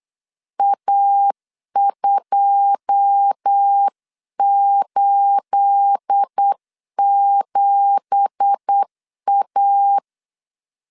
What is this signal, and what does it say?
Morse code, heard as a keyed tone, about 8 wpm, 800 Hz. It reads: A287A
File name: CW / DET / 1 9 1500C S A 8wpm 5637